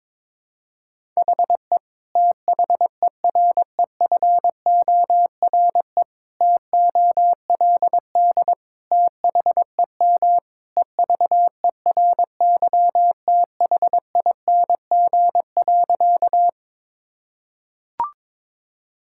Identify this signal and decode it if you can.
Morse code, heard as a keyed tone, about 22 wpm, 700 Hz. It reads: HE THEREFORE TOLD THEM EVERYTHING.
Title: Sentences from Top 1000 Words - Review Part 5 - 22wpm